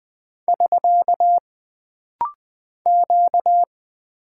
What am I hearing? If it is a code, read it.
Morse code, heard as a keyed tone, about 20 wpm, 700 Hz. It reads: <SK> Q
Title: Single Character (Letter:Number:Punctuation:Prosign) - Rapid-Fire - 20wpm